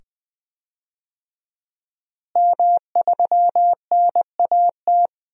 Morse code, heard as a keyed tone, about 20 wpm, 700 Hz. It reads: M3NAT
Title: DXCC-100 Call Signs - 5 Characters - Repeated 2x - 20wpm